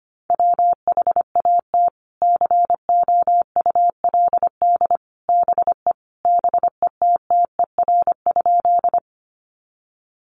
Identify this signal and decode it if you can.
Morse code, heard as a keyed tone, about 25 wpm, 700 Hz. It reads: WHAT COULD BE BETTER?
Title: Sentences from Top 400 Words 25wpm